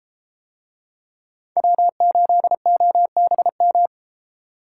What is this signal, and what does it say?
Morse code, heard as a keyed tone, about 30 wpm, 700 Hz. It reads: W8OBM